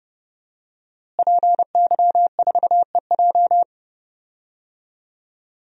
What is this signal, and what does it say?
Morse code, heard as a keyed tone, about 30 wpm, 700 Hz. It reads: PY4EJ
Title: DXCC-100 Call Signs - 5 Characters - Repeated 2x - 30wpm